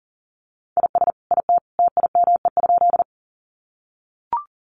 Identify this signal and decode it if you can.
Morse code, heard as a keyed tone, about 40 wpm, 700 Hz. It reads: IS IT TIME?
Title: Sentences from Top 100 Words 40wpm